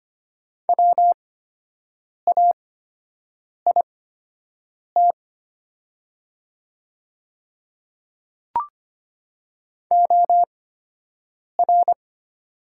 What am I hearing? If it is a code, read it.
Morse code, heard as a keyed tone, about 25 wpm, spaced as for 7 wpm, 690 Hz. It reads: WAIT OR
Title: 8x Character Spacing - Top 500 Words - 25wpm